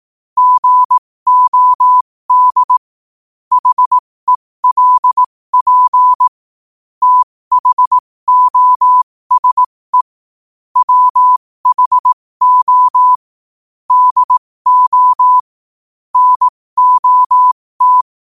Morse code, heard as a keyed tone, about 18 wpm, spaced as for 15 wpm, 1000 Hz. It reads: GOD HELP THOSE WHO DO NOT
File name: qotdfiles / 15wpm